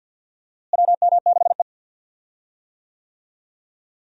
Morse code, heard as a keyed tone, about 52 wpm, 700 Hz. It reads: WM6E
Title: USA Call Signs - 4 Characters - Repeated 2x - 50wpm